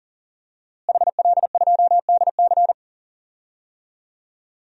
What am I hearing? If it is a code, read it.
Morse code, heard as a keyed tone, about 37 wpm, 700 Hz. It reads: HL2DC
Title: DXCC-100 Call Signs - 5 Characters - Repeated 2x - 40wpm